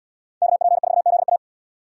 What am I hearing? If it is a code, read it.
Morse code, heard as a keyed tone, about 74 wpm, 700 Hz. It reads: KP4ZA